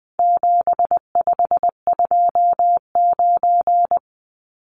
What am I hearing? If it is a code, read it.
Morse code, heard as a keyed tone, about 19 wpm, 700 Hz. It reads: 7529
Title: Numbers - 4-Digit 20wpm